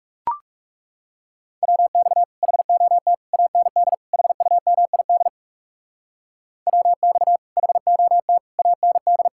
Morse code, heard as a keyed tone, about 38 wpm, 700 Hz. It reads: WX HOT AND HUMID WX HOT AND
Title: Speed-Racing Common QSO Phrases - 30wpm